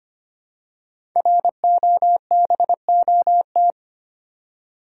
Morse code, heard as a keyed tone, about 25 wpm, 700 Hz. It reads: ROBOT
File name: 5 Letter Words - First 3 Letters Given - 25wpm